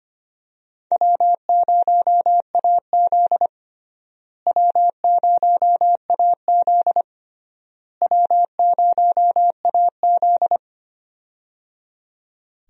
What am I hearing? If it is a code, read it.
Morse code, heard as a keyed tone, about 25 wpm, 700 Hz. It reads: W0AZ W0AZ W0AZ
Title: USA Call Signs - 4 Characters - Repeated 3x - 25wpm